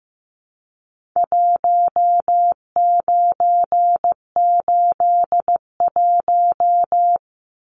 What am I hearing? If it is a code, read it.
Morse code, heard as a keyed tone, about 15 wpm, 700 Hz. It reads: 1981
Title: Numbers - Contest Style - 4-Digit 15wpm